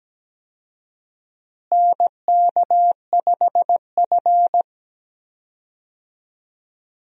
Morse code, heard as a keyed tone, about 17 wpm, 700 Hz. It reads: NK5F
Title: USA Call Signs - 4 Characters - Repeated 3x - 17wpm